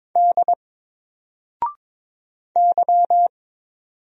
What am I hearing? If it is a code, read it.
Morse code, heard as a keyed tone, about 22 wpm, 700 Hz. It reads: D Y
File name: Single-Letter-Number-22wpm